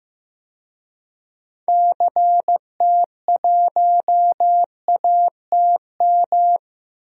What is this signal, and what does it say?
Morse code, heard as a keyed tone, about 15 wpm, 700 Hz. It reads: CT1ATM
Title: DXCC-100 Call Signs - 6 Characters - Repeated 2x - 15wpm